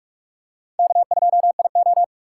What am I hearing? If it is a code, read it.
Morse code, heard as a keyed tone, about 47 wpm, 700 Hz. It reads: K2IO